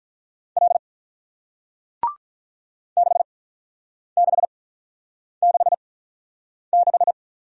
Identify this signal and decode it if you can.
Morse code, heard as a keyed tone, about 48 wpm, spaced as for 9 wpm, 700 Hz. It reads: L 6666